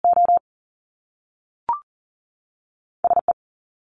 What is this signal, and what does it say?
Morse code, heard as a keyed tone, about 39 wpm, 700 Hz. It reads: O S E
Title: Top Two Letter Combinations in Words 40wpm